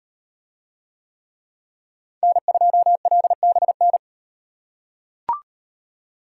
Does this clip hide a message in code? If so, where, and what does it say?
Yes: Morse code from 2.2 to 4.0 s, heard as a keyed tone, about 34 wpm, 700 Hz. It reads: N2LBN